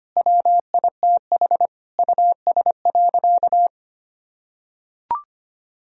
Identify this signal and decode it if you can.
Morse code, heard as a keyed tone, about 25 wpm, 700 Hz. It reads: WITH US.
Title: Sentences from Top 1000 Words - Review Part 3 - 25wpm